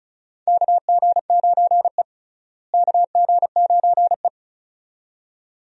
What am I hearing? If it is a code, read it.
Morse code, heard as a keyed tone, about 34 wpm, 700 Hz. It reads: KG9E KG9E
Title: USA Call Signs - 4 Characters - Repeated 2x - 35wpm